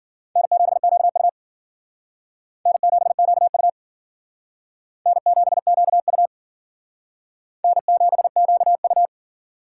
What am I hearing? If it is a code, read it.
Morse code, heard as a keyed tone, about 50 wpm, 700 Hz. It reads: N7QU N7QU N7QU N7QU
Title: Speed Racing - USA 1x2 Call Signs 40wpm